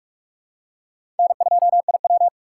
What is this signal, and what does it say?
Morse code, heard as a keyed tone, about 46 wpm, 700 Hz. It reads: N2IW
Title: USA Call Signs - 4 Characters - 4x Character Spacing - 45wpm